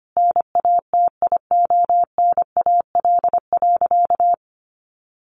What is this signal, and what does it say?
Morse code, heard as a keyed tone, about 25 wpm, 700 Hz. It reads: NATIONAL.
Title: Sentences from Top 1200 Words 25wpm